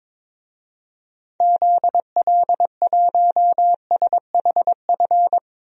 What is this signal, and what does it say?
Morse code, heard as a keyed tone, about 22 wpm, 700 Hz. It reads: ZL1SHF
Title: DXCC-100 Call Signs - 6 Characters - 22wpm